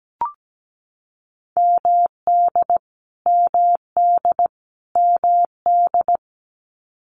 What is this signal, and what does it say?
Morse code, heard as a keyed tone, about 17 wpm, 700 Hz. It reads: MD MD MD
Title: US States Repeated 3x 17wpm